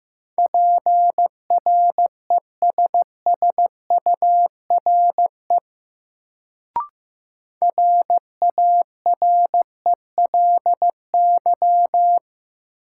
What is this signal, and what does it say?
Morse code, heard as a keyed tone, about 15 wpm, 700 Hz. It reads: PRESSURE RARELY
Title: Sentences from Top 1400 Words - New Word Drill Alt - 15wpm